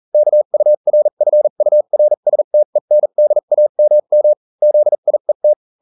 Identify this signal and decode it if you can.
Morse code, heard as a keyed tone, about 40 wpm, spaced as for 35 wpm, 600 Hz. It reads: KURFURSTENDAMM ZIET